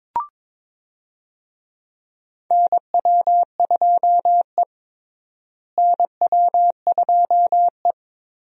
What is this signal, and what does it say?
Morse code, heard as a keyed tone, about 21 wpm, 700 Hz. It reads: NW2E NW2E
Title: USA Call Signs - 4 Characters - Repeated 3x - 22wpm